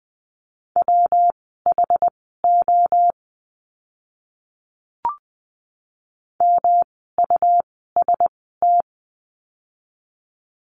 Morse code, heard as a keyed tone, about 20 wpm, spaced as for 14 wpm, 700 Hz. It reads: WHO MUST
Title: 237 Lesson 38f - Letter q - Words 20wpm